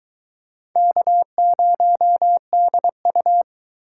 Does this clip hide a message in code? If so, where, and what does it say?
Yes: Morse code from 0.8 to 3.4 s, heard as a keyed tone, about 22 wpm, 700 Hz. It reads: K0DU